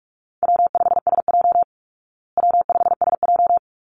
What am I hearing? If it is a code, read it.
Morse code, heard as a keyed tone, about 44 wpm, 700 Hz. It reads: W5SJ W5SJ
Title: Easy Call Signs Repeated 2x - 45wpm